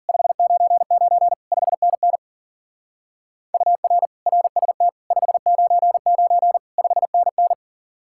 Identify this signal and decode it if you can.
Morse code, heard as a keyed tone, about 50 wpm, 700 Hz. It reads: 599 5NN UR RST 599 5NN